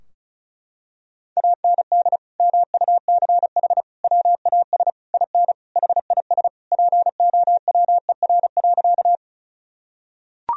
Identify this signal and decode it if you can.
Morse code, heard as a keyed tone, about 36 wpm, 700 Hz. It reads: AND MUCH WAS IN HIS POWER.
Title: Sentences from Top 1000 Words - Review Part 2 - 35wpm